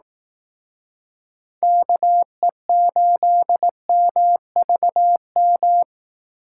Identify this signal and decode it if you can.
Morse code, heard as a keyed tone, about 18 wpm, 700 Hz. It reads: KE8MVM
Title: Speed Racing - USA 2x3 Call Signs 15wpm